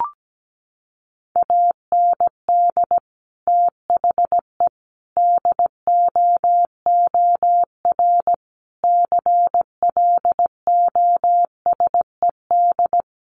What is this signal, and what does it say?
Morse code, heard as a keyed tone, about 17 wpm, 700 Hz. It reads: AND THE DOOR CLOSED
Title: Sentences from Top 1400 Words 17wpm